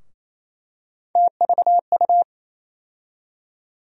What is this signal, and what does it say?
Morse code, heard as a keyed tone, about 28 wpm, 700 Hz. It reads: TVU
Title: USA Call Sign Postfixes - 3 Letters 28wpm